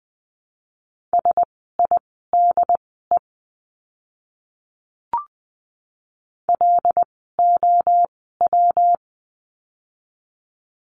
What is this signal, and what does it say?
Morse code, heard as a keyed tone, about 20 wpm, spaced as for 14 wpm, 700 Hz. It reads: SIDE LOW
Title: 115 Lesson 19f - Letter w - Words 20wpm